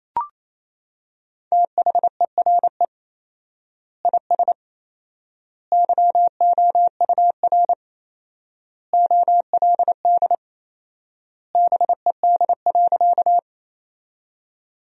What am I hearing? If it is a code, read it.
Morse code, heard as a keyed tone, about 28 wpm, 700 Hz. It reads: THERE IS YOUR OLD BED.
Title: 4x Word-Spacing - Sentences from Top 500 Words - 28wpm